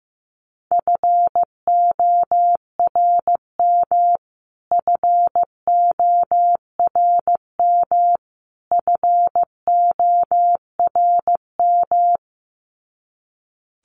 Morse code, heard as a keyed tone, about 15 wpm, 700 Hz. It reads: FORM FORM FORM
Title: Top 500 Words Repeated 15wpm